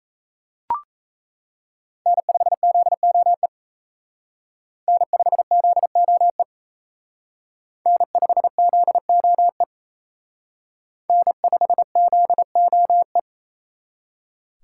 Morse code, heard as a keyed tone, about 34 wpm, 700 Hz. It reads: N5ZOE N5ZOE N5ZOE N5ZOE
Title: Speed Racing - USA Call Signs 28wpm